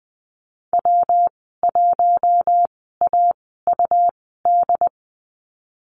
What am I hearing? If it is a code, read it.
Morse code, heard as a keyed tone, about 20 wpm, 700 Hz. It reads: W 1 A U D
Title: 154 Lesson 25g - Letter p - Call Signs 20wpm